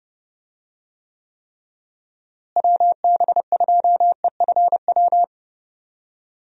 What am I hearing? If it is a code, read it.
Morse code, heard as a keyed tone, about 30 wpm, 700 Hz. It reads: WB2EFW
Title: USA Call Signs - 6 Characters - Repeated 3x - 30wpm